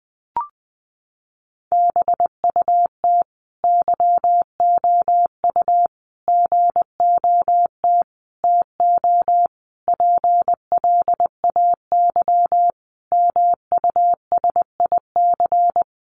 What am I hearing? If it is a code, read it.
Morse code, heard as a keyed tone, about 20 wpm, 700 Hz. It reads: BUT YOU GOT TO PLAY MUSIC
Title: Sentences from Top 300 Words 20wpm